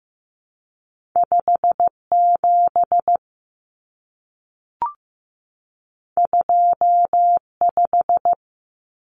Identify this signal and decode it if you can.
Morse code, heard as a keyed tone, about 15 wpm, 700 Hz. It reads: 5 7 2 5
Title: Common Cut Numbers - 2-Digit 15wpm